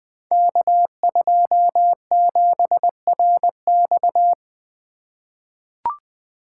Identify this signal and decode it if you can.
Morse code, heard as a keyed tone, about 20 wpm, 700 Hz. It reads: K27RX
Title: Code Groups - 5 Letter:Numbers - 20wpm